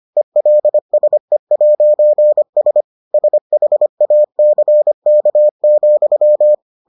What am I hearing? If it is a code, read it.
Morse code, heard as a keyed tone, about 25 wpm, 600 Hz. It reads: ELSE'S SHACK,